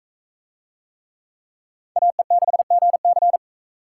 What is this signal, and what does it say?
Morse code, heard as a keyed tone, about 40 wpm, 700 Hz. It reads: AE6GC